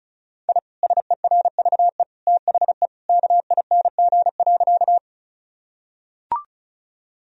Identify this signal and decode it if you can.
Morse code, heard as a keyed tone, about 34 wpm, 700 Hz. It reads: I SERVE THE KING.
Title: Sentences from Top 1000 Words - Review Part 2 - 35wpm